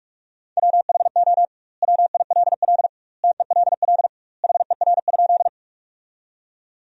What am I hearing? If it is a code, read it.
Morse code, heard as a keyed tone, about 46 wpm, 700 Hz. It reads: WHO WILL TELL HER?
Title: 2x Word-Spacing - Sentences from Top 200 Words - 45wpm